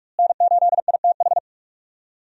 Morse code, heard as a keyed tone, about 46 wpm, 700 Hz. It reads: N8ITH
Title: USA Call Signs - 5 Characters - Repeated 3x - 45wpm